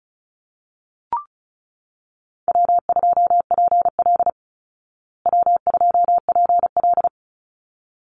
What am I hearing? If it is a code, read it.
Morse code, heard as a keyed tone, about 34 wpm, 700 Hz. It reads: W2PL W2PL
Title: Easy Call Signs Repeated 2x - 35wpm